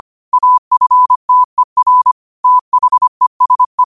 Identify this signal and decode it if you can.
Morse code, heard as a keyed tone, about 25 wpm, 1000 Hz. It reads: AFTER THESE
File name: qotdfiles / 25wpm